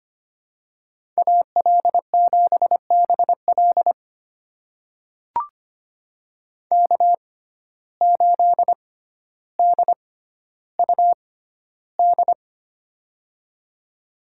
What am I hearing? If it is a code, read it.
Morse code, heard as a keyed tone, about 25 wpm, spaced as for 13 wpm, 690 Hz. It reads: AL7BL K8DUD
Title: USA Call Signs - 5 Characters - 6x Character Spacing - 25wpm